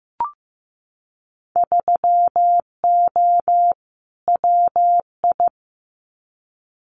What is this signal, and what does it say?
Morse code, heard as a keyed tone, about 15 wpm, 700 Hz. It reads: 3O WI
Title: Winter Field Day Exchanges 15wpm